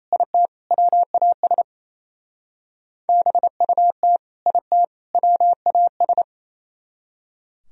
Morse code, heard as a keyed tone, about 34 wpm, 700 Hz. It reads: IT WAS BUT IT WAS
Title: Speed-Racing - Sets of 3 Words 28wpm